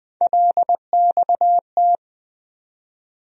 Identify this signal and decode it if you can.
Morse code, heard as a keyed tone, about 20 wpm, 700 Hz. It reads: LXT
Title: USA Call Sign Postfixes - 3 Letters 20wpm